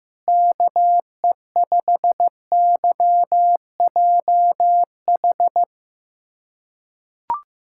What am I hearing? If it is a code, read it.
Morse code, heard as a keyed tone, about 15 wpm, 700 Hz. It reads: KE5YJH
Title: USA Callsign DE Callsign 15wpm